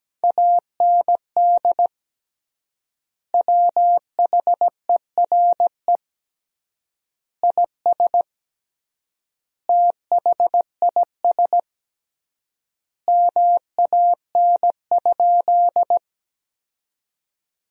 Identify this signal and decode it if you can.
Morse code, heard as a keyed tone, about 17 wpm, 700 Hz. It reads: AND WHERE IS THIS MAN?
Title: 3x Word-Spacing - Sentences from Top 200 Words - 17wpm